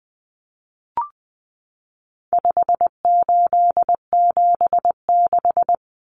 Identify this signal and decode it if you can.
Morse code, heard as a keyed tone, about 19 wpm, 700 Hz. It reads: 5876
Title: Numbers - 4-Digit 20wpm